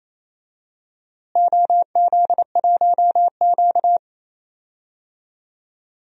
Morse code, heard as a keyed tone, about 28 wpm, 700 Hz. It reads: OZ1Q
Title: DXCC-100 Call Signs - 4 Characters - Repeated 3x - 28wpm